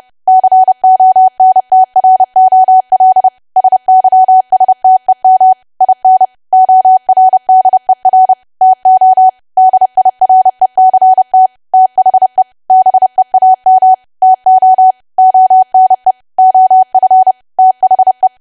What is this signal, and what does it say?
Morse code, heard as a keyed tone, about 30 wpm, 750 Hz. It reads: CONTROL SYSTEM IN ORDER TO DIRECT THE BEAM TO ONE OF THE
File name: cwa / W1AW-Level3 / 30WPM